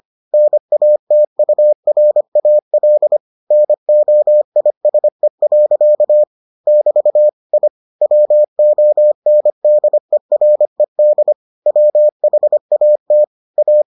Morse code, heard as a keyed tone, about 25 wpm, 600 Hz. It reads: NATURAL NOISE. <BT> I WONDERED WHAT A